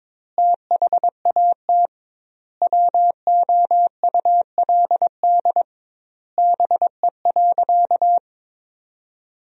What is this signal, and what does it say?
Morse code, heard as a keyed tone, about 22 wpm, 700 Hz. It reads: THAT WOULD BE.
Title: 2x Word-Spacing - Sentences from Top 200 Words - 22wpm